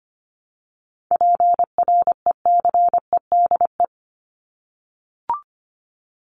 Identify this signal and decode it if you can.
Morse code, heard as a keyed tone, about 25 wpm, 700 Hz. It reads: PRECEDE
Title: ING Suffix - Top 100 Words with Base Words 25wpm